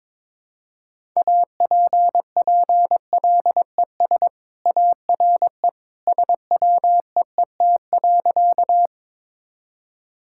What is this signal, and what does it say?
Morse code, heard as a keyed tone, about 22 wpm, 700 Hz. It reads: APPLES ARE SWEET.
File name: practice / 22wpm / Easy Sentences with Hints - 22wpm